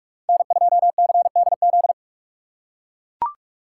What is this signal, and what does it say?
Morse code, heard as a keyed tone, about 40 wpm, 700 Hz. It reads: N2CDZ